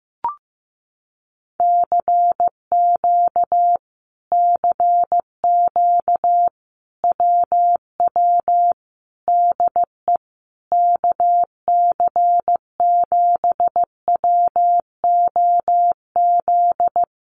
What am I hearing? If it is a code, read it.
Morse code, heard as a keyed tone, about 15 wpm, 700 Hz. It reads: CQ CQ WW DE KC7WOZ
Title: Common QSO Phrases Encore 15wpm